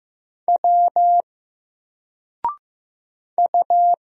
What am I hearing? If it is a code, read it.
Morse code, heard as a keyed tone, about 15 wpm, 700 Hz. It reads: W U
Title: Single Letter-Number 15wpm